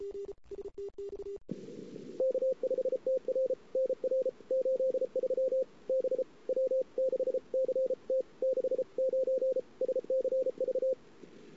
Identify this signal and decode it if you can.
Morse code, heard as a keyed tone, about 33 wpm, 520 Hz. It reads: K5TR NR 83 B W6CT 69 SCV
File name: CW / 2008-ss-cw-k5tr-op-n5rz -23:00